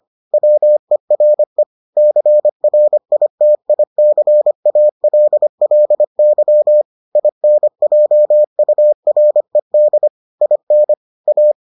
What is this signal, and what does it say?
Morse code, heard as a keyed tone, about 25 wpm, 600 Hz. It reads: WERE CRITICALLY INJURED IN A